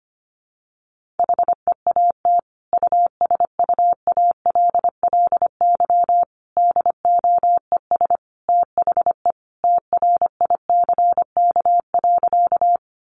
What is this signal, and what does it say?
Morse code, heard as a keyed tone, about 25 wpm, 700 Hz. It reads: HEAT USUALLY DOES THE TRICK.
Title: Sentences from Top 1500 Words 25wpm